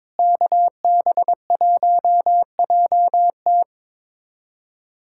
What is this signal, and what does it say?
Morse code, heard as a keyed tone, about 22 wpm, 700 Hz. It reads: KB1JT
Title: USA Callsign DE Callsign - 5 Characters - 22wpm